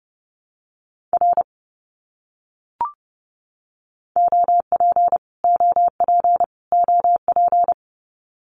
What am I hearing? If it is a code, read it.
Morse code, heard as a keyed tone, about 30 wpm, 700 Hz. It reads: R OP OP OP
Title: Abbreviations Repeated 3x - 30wpm